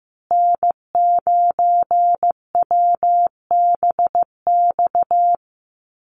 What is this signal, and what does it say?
Morse code, heard as a keyed tone, about 15 wpm, 700 Hz. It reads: N9WBX
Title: USA Call Signs Repeated 2x 15wpm